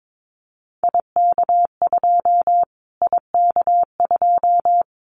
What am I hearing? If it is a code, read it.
Morse code, heard as a keyed tone, about 22 wpm, 700 Hz. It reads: IK2 IK2
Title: Callsign Prefixes for Top 25 DXCC - Repeated 3x - 22wpm